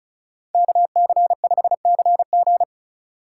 Morse code, heard as a keyed tone, about 43 wpm, 700 Hz. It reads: KC5CG
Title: Speed Racing - USA 2x2 Call Signs 35wpm